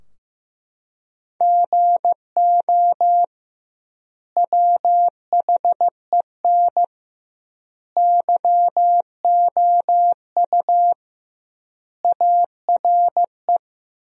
Morse code, heard as a keyed tone, about 15 wpm, 700 Hz. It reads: GO WHEN YOU ARE
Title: 2x Word-Spacing - Sentences from Top 500 Words - 15wpm